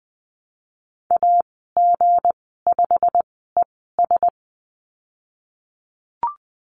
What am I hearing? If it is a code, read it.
Morse code, heard as a keyed tone, about 19 wpm, spaced as for 14 wpm, 700 Hz. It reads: AG5ES
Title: 161 Lesson 26g - Letter g - Call Signs 20wpm